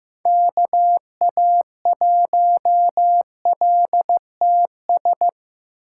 Morse code, heard as a keyed tone, about 15 wpm, 700 Hz. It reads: KA1LTS
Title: USA Call Signs - 3x Character Spacing - 15wpm